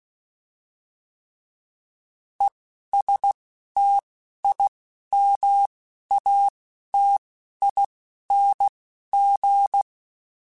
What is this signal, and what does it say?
Morse code, heard as a keyed tone, about 16 wpm, spaced as for 11 wpm, 800 Hz. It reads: ESTIMATING
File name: Alphabetical / Alphabetical mp3s / Es @10